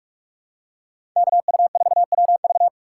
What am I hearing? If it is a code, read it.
Morse code, heard as a keyed tone, about 47 wpm, 700 Hz. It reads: KU4WV
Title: USA Call Signs - 5 Characters - 2x Character Spacing - 45wpm